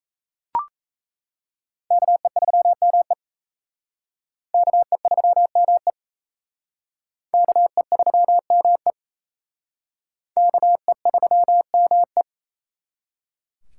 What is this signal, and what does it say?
Morse code, heard as a keyed tone, about 34 wpm, 700 Hz. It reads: KE3ME KE3ME KE3ME KE3ME
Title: Speed Racing - USA 2x2 Call Signs 28wpm